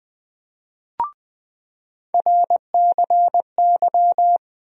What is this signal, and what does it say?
Morse code, heard as a keyed tone, about 20 wpm, 700 Hz. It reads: RCY
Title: USA Call Sign Postfixes - 3 Letters 20wpm